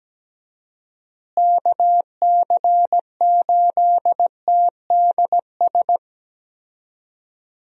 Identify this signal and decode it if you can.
Morse code, heard as a keyed tone, about 17 wpm, 700 Hz. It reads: KC8TDS
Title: USA Call Signs - 6 Characters - Repeated 3x - 17wpm